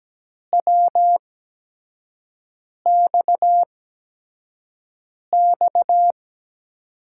Morse code, heard as a keyed tone, about 17 wpm, spaced as for 5 wpm, 700 Hz. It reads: WXX
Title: USA Call Signs - 5 Characters - 8x Character Spacing - 17wpm